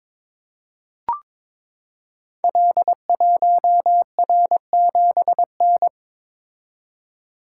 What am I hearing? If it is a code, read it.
Morse code, heard as a keyed tone, about 22 wpm, 700 Hz. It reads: L1R7N
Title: Code Groups - 5 Letter:Numbers - 22wpm